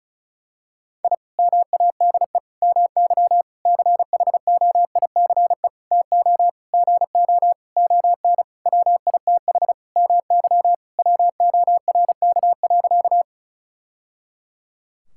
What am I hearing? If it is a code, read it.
Morse code, heard as a keyed tone, about 36 wpm, 700 Hz. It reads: I MADE MY CHOICE TO GO ON WITH MY WORK.
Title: Sentences from Top 1000 Words - Review Part 6 - 35wpm